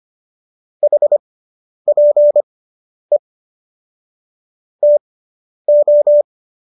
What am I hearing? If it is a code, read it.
Morse code, heard as a keyed tone, about 25 wpm, spaced as for 10 wpm, 590 Hz. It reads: HPE TO